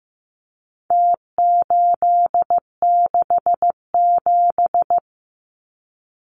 Morse code, heard as a keyed tone, about 15 wpm, 700 Hz. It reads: T867
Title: Numbers - Contest Style - 4-Digit 15wpm